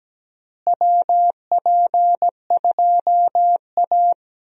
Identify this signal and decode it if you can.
Morse code, heard as a keyed tone, about 17 wpm, 700 Hz. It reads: WP2A
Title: USA Call Signs - 4 Characters - Repeated 3x - 17wpm